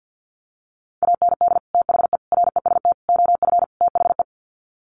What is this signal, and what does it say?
Morse code, heard as a keyed tone, about 50 wpm, 700 Hz. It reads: AND THE REST OF THE
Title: Sets of 5 Words 50wpm